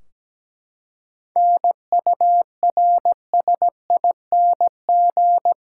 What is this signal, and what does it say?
Morse code, heard as a keyed tone, about 17 wpm, 700 Hz. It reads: NURSING
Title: Practice QSO - Exchange 3 - More Info - 17wpm